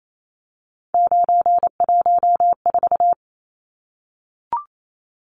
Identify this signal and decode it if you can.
Morse code, heard as a keyed tone, about 28 wpm, 700 Hz. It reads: 914
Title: Numbers - 3-Digit 28wpm